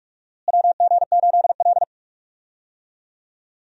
Morse code, heard as a keyed tone, about 42 wpm, 700 Hz. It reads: WG8L
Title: USA Call Signs - 4 Characters - Repeated 3x - 45wpm